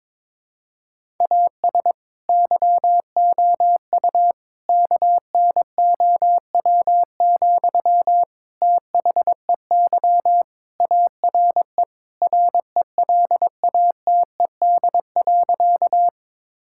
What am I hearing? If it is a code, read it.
Morse code, heard as a keyed tone, about 22 wpm, 700 Hz. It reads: AS YOU KNOW, THEY ARE RELATED.
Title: Sentences from Top 1000 Words - Review Part 5 - 22wpm